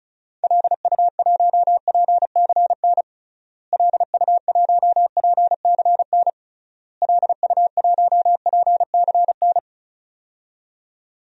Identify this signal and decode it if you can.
Morse code, heard as a keyed tone, about 36 wpm, 700 Hz. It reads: LU1PCN LU1PCN LU1PCN
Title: DXCC-100 Call Signs - 6 Characters - Repeated 3x - 35wpm